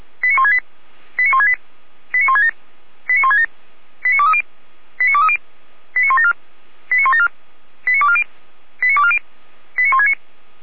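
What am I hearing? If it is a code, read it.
Five-tone selective call (ZVEI), heard as quick runs of five tones, tones of about 80 ms. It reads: 89168 89168 89167 89167 89230 89230 89174 89174 89240 89240 89169
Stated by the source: ZVEI five-tone call 89163 received by SDR.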